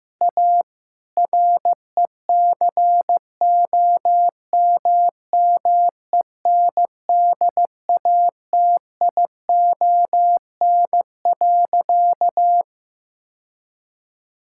Morse code, heard as a keyed tone, about 15 wpm, 700 Hz. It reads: A RECOMMENDATION.
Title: Sentences from Top 1600 Words 15wpm